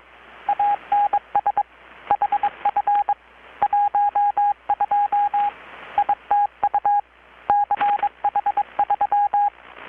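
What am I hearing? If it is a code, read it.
Morse code, heard as a keyed tone, about 22 wpm, 820 Hz, with radio noise. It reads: ANS HF 12 ITU CH3
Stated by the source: Morse from KPH's Night of Nights XX maritime radiotelegraphy event, received in Chiapas, Mexico about 3,800 km away on a Kenwood R-600 with a grounded 10 m long-wire antenna.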